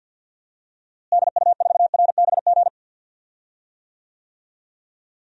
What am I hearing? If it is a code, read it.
Morse code, heard as a keyed tone, about 51 wpm, 700 Hz. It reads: DU4RBG
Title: DXCC-100 Call Signs - 6 Characters - Repeated 3x - 50wpm